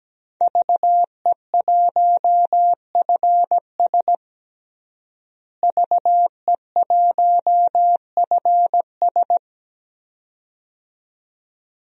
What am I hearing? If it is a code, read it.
Morse code, heard as a keyed tone, about 17 wpm, 700 Hz. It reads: VE1FS VE1FS
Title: DXCC-100 Call Signs - 5 Characters - Repeated 2x - 17wpm